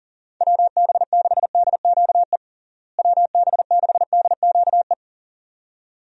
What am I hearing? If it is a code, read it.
Morse code, heard as a keyed tone, about 40 wpm, 700 Hz. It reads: WB6DQE WB6DQE
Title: USA Call Signs - 6 Characters - Repeated 2x - 40wpm